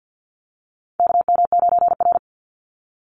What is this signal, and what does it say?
Morse code, heard as a keyed tone, about 50 wpm, 700 Hz. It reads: KM9R
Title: Call Signs - 50wpm